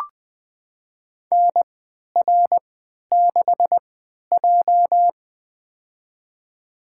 Morse code, heard as a keyed tone, about 20 wpm, spaced as for 11 wpm, 700 Hz. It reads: NR6J
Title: USA Call Signs - 4 Characters - 3x Character Spacing - 20wpm